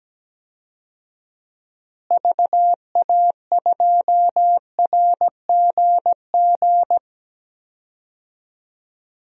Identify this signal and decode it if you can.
Morse code, heard as a keyed tone, about 17 wpm, 700 Hz. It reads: VA2RGG
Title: DXCC-100 Call Signs - 6 Characters - Repeated 2x - 17wpm